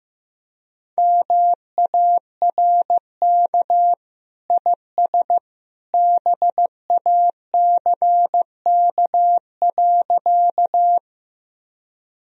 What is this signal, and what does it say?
Morse code, heard as a keyed tone, about 15 wpm, 700 Hz. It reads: MARK IS BACK.
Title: Sentences from Top 1000 Words - Review Part 1 - 15wpm